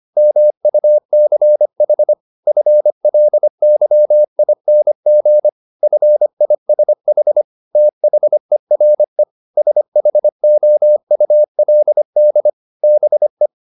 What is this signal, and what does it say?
Morse code, heard as a keyed tone, about 25 wpm, 600 Hz. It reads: MUCH FLYING FISH THERE SHOULD BE